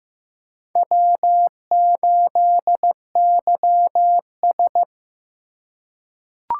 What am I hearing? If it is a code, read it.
Morse code, heard as a keyed tone, about 15 wpm, 700 Hz. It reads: W8YS
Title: USA Call Signs - 4 Characters - 8x Character Spacing - 15wpm